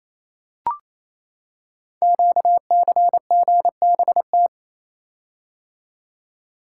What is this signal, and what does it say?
Morse code, heard as a keyed tone, about 27 wpm, 700 Hz. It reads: QCGBT
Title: Code Groups - 5 Letter:Numbers - 28wpm